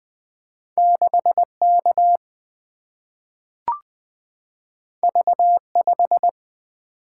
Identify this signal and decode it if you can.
Morse code, heard as a keyed tone, about 20 wpm, 700 Hz. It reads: 6 K V 5
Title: Two Letter-Number Combinations - 20wpm